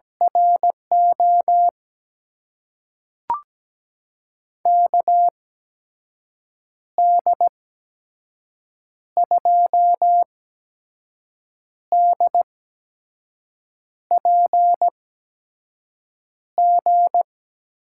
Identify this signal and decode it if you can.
Morse code, heard as a keyed tone, about 17 wpm, spaced as for 5 wpm, 700 Hz. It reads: RO KD2DPG
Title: USA Call Signs - 6 Characters - 8x Character Spacing - 17wpm